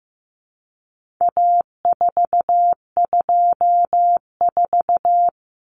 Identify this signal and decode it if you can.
Morse code, heard as a keyed tone, about 15 wpm, 700 Hz. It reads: A424
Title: Common Cut Numbers - 4-Digit 15wpm